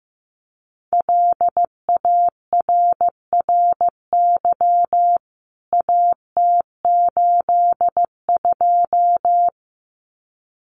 Morse code, heard as a keyed tone, about 15 wpm, 700 Hz. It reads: LARRY AT82
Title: CWT Exchange 15wpm